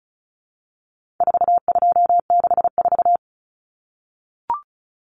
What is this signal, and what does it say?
Morse code, heard as a keyed tone, about 34 wpm, 700 Hz. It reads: 4264
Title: Common Cut Numbers - 4-Digit 35wpm